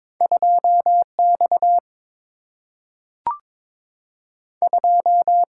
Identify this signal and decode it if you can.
Morse code, heard as a keyed tone, about 22 wpm, 700 Hz. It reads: 2 X 2
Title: Single-Letter Single-Number 22wpm